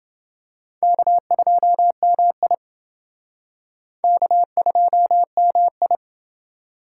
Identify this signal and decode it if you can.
Morse code, heard as a keyed tone, about 28 wpm, 700 Hz. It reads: K2MI K2MI